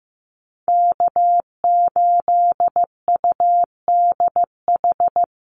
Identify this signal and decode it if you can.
Morse code, heard as a keyed tone, about 15 wpm, 700 Hz. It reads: K8UDH
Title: Call Signs Repeated 3x - 15wpm